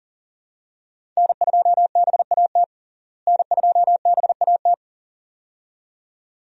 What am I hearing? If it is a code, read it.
Morse code, heard as a keyed tone, about 37 wpm, 700 Hz. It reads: N2BAT N2BAT
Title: USA Call Signs - 5 Characters - Repeated 2x - 40wpm